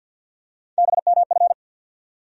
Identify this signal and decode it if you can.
Morse code, heard as a keyed tone, about 47 wpm, 700 Hz. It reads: BMF
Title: USA Call Sign Postfixes - 3 Letters 50wpm